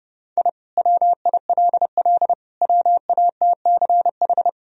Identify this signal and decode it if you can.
Morse code, heard as a keyed tone, about 32 wpm, 700 Hz. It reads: I WILL WATCH
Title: Sentences from Top 1000 Words - Review Part 1 - 30wpm